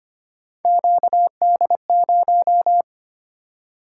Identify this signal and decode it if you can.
Morse code, heard as a keyed tone, about 25 wpm, 700 Hz. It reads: QD0
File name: Three Letter-Number Combinations - 25wpm